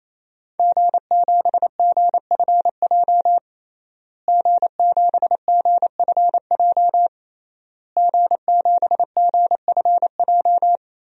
Morse code, heard as a keyed tone, about 28 wpm, 700 Hz. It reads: G7GFJ G7GFJ G7GFJ
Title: DXCC-100 Call Signs - 5 Characters - Repeated 3x - 28wpm